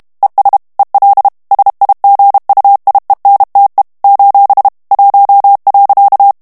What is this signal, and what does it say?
Morse code, heard as a keyed tone, about 32 wpm, 800 Hz. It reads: ES EL SIGUIENTE : 1.
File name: morse mp3 / ESPAGNOL TEXTE EN CLAIR 1 VITESSE 30 MOTS MINUTES